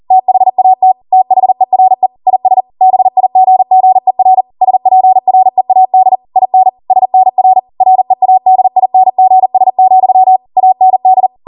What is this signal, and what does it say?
Morse code, heard as a keyed tone, about 40 wpm, 750 Hz. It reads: THAT THERE IS BIGGER SPREAD IN SNR READINGS, AND